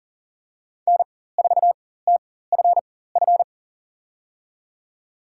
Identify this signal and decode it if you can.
Morse code, heard as a keyed tone, about 37 wpm, spaced as for 18 wpm, 700 Hz. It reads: N4TFF
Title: USA Call Signs - 4x Character Spacing - 40wpm